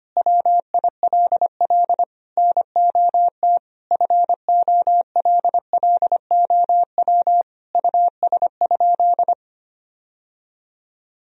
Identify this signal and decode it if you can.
Morse code, heard as a keyed tone, about 25 wpm, 700 Hz. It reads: WILL NOT FOLLOW US?
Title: Sentences from Top 1000 Words - Review Part 1 - 25wpm